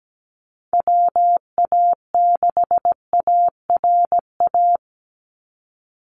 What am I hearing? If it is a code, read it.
Morse code, heard as a keyed tone, about 17 wpm, 700 Hz. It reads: WA6ARA
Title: NA SOTA Chaser Callsigns 17wpm